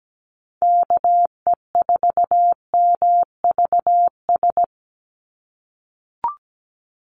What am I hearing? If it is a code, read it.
Morse code, heard as a keyed tone, about 17 wpm, 700 Hz. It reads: KE4MVS
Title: USA Call Signs 17wpm